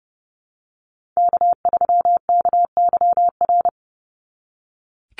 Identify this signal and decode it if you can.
Morse code, heard as a keyed tone, about 30 wpm, 700 Hz. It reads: K3KYR
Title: NA SOTA Chaser Callsigns 30wpm